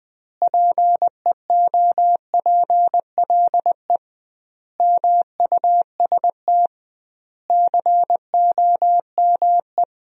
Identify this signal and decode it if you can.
Morse code, heard as a keyed tone, about 21 wpm, 700 Hz. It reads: PEOPLE MUST COME
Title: 2x Word-Spacing - Sentences from Top 300 Words - 20wpm